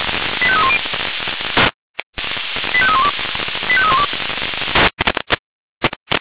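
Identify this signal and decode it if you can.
Five-tone selective call (ZVEI), heard as quick runs of five tones, tones of about 70 ms. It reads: 85310 85312 85312
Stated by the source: SDR-received ZVEI five-tone selective call logged by audio.f85360.feuerwehr-orlamuende.org, fire brigade Orlamünde.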